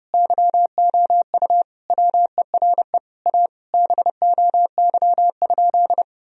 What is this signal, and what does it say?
Morse code, heard as a keyed tone, about 30 wpm, 700 Hz. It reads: YOU WERE A BOY?
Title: Sentences from Top 1000 Words - Review Part 1 - 30wpm